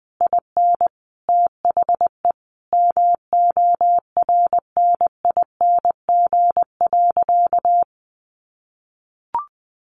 Morse code, heard as a keyed tone, about 19 wpm, 700 Hz. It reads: IN THE MORNING.
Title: Sentences from Top 1400 Words 20wpm